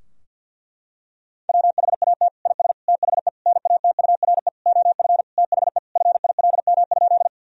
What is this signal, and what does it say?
Morse code, heard as a keyed tone, about 50 wpm, 700 Hz. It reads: WHAT IS THE NATURE OF THE FILM?
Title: Sentences from Top 1000 Words - Review Part 3 - 50wpm